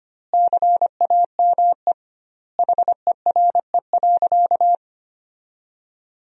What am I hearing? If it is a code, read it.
Morse code, heard as a keyed tone, about 25 wpm, 700 Hz. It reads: CAME HERE.
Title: 2x Word-Spacing - Sentences from Top 500 Words - 25wpm